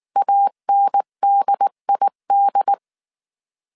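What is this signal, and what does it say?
Morse code, heard as a keyed tone, about 19 wpm, 800 Hz. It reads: ANDID
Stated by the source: Morse code dictation at 18 wpm from the IW3SGT QRS Morse code training course, five-character groups of 300 characters.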